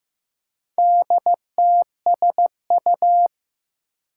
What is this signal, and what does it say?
Morse code, heard as a keyed tone, about 15 wpm, 700 Hz. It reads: DTSU
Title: Four Challenging Letter-Number Combinations 15wpm